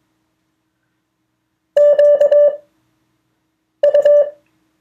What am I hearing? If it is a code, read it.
Morse code, heard as a keyed tone, about 21 wpm, spaced as for 6 wpm, 580 Hz. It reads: QU
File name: 2of4-words-1